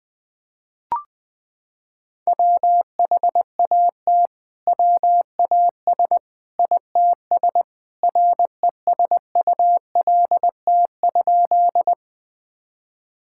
Morse code, heard as a keyed tone, about 21 wpm, 700 Hz. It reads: WHAT WAS ITS RESULT?
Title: Sentences from Top 1000 Words - Review Part 5 - 20wpm